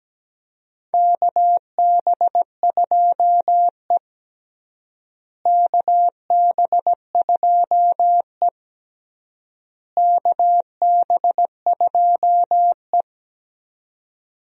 Morse code, heard as a keyed tone, about 17 wpm, 700 Hz. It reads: KB2E KB2E KB2E
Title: USA Call Signs - 4 Characters - Repeated 3x - 17wpm